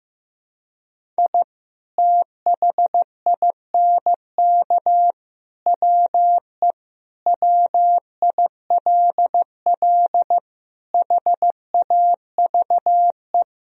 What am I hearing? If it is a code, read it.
Morse code, heard as a keyed tone, about 15 wpm, 700 Hz. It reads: I THINK WE WILL HAVE
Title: Sentences from Top 1000 Words - Review Part 1 - 15wpm